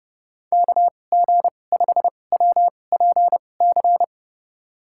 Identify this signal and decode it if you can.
Morse code, heard as a keyed tone, about 30 wpm, spaced as for 22 wpm, 700 Hz. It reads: KG5WPC
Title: USA Call Signs - 6 Characters - 2x Character Spacing - 30wpm